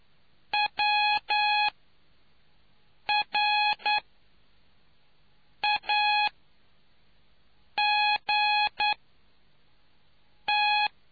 Morse code, heard as a keyed tone, about 9 wpm, spaced as for 4 wpm, 840 Hz. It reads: WRAGT